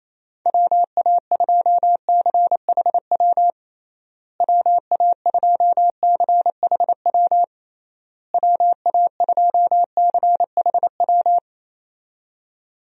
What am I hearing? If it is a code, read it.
Morse code, heard as a keyed tone, about 28 wpm, 700 Hz. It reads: WA2CHW WA2CHW WA2CHW
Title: USA Call Signs - 6 Characters - Repeated 3x - 28wpm